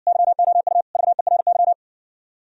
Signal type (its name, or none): Morse code